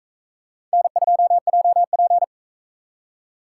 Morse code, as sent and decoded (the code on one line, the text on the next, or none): -. ..--- .--- .--.
N2JP